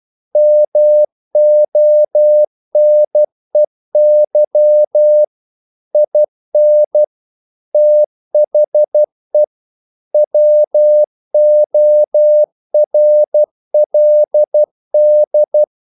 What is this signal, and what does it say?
Morse code, heard as a keyed tone, about 12 wpm, 600 Hz. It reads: MONEY IN THE WORLD
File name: CW / Practice / EnglishQuotations0312